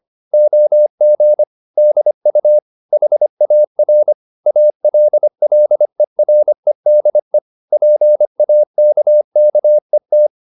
Morse code, heard as a keyed tone, about 25 wpm, 600 Hz. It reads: OG DU HAR ALLEREDE PAKKET
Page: Ferie 25 wpm